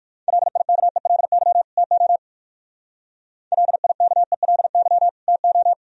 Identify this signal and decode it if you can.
Morse code, heard as a keyed tone, about 55 wpm, 700 Hz. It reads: LIKELY TO LIKELY TO
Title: Speed-Racing - Sets of 2 Words 45wpm